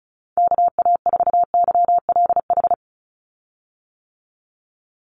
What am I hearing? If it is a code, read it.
Morse code, heard as a keyed tone, about 34 wpm, 700 Hz. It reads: KA4YLH
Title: USA Call Signs 35wpm